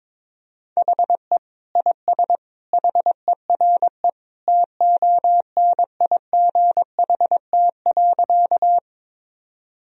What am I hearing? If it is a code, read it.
Morse code, heard as a keyed tone, about 22 wpm, 700 Hz. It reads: HE IS HERE TONIGHT.
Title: Sentences from Top 1000 Words - Review Part 3 - 22wpm